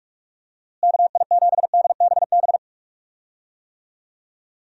Morse code, heard as a keyed tone, about 43 wpm, 700 Hz. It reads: KI7DBB
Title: USA Call Signs - 6 Characters - Repeated 2x - 45wpm